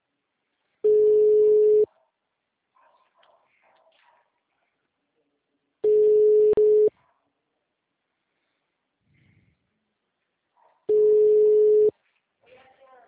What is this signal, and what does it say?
Ringback tone (Europe, 425 Hz), with speech.
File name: out- -2036- - - .8308